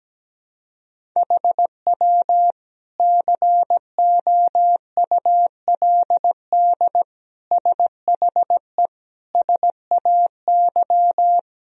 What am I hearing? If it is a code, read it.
Morse code, heard as a keyed tone, about 17 wpm, 700 Hz. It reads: HW COULD SHE SAY
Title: Sentences from Top 200 Words and Abbreviations 17wpm